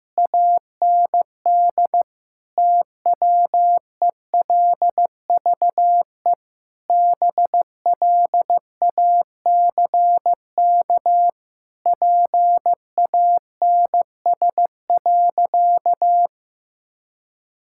Morse code, heard as a keyed tone, about 15 wpm, 700 Hz. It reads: AND TWELVE BLACK PANS.
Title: Pangram Challenge v2 - 15wpm